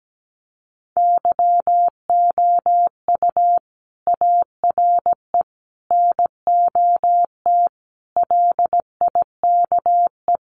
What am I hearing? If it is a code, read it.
Morse code, heard as a keyed tone, about 17 wpm, 700 Hz. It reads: YOU ARE NOT LIKE
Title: Sentences from Top 600 Words 17wpm